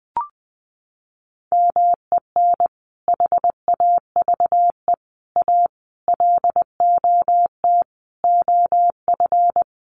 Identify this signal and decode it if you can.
Morse code, heard as a keyed tone, about 20 wpm, 700 Hz. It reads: MEN HAVE A LOT OF
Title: Sentences from Top 1400 Words 20wpm